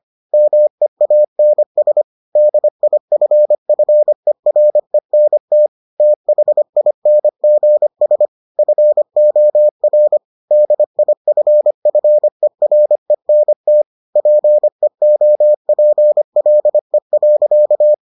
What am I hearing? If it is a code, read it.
Morse code, heard as a keyed tone, about 25 wpm, 600 Hz. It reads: MEANS DIFFERENT THINGS FOR DIFFERENT PEOPLE.